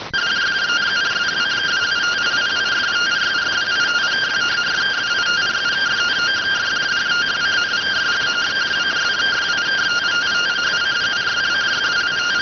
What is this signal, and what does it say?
Two-tone teleprinter signal, about 170 Hz shift.